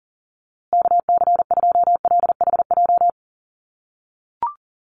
Morse code, heard as a keyed tone, about 37 wpm, 700 Hz. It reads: KC2LHJ
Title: USA Call Signs Repeated 2x 40wpm